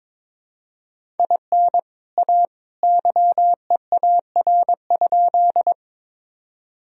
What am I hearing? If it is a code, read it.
Morse code, heard as a keyed tone, about 22 wpm, 700 Hz. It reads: IN A YEAR?
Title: Sentences from Top 1000 Words - Review Part 1 - 22wpm